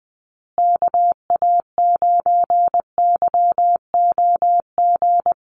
Morse code, heard as a keyed tone, about 20 wpm, 700 Hz. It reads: KA9YOG
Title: USA Call Signs Repeated 2x 20wpm